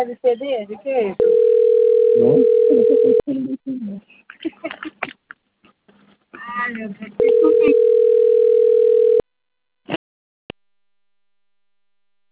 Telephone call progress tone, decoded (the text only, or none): ringback tone